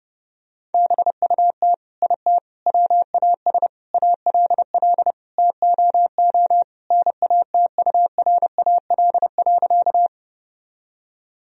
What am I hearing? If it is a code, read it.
Morse code, heard as a keyed tone, about 30 wpm, 700 Hz. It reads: BUT IT WAS ALL TOO NATURAL.
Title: Sentences from Top 1000 Words - Review Part 4 - 30wpm